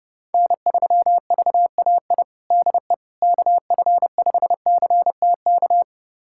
Morse code, heard as a keyed tone, about 30 wpm, 700 Hz. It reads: N3VAI DE KF5CTK
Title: USA Callsign DE Callsign 30wpm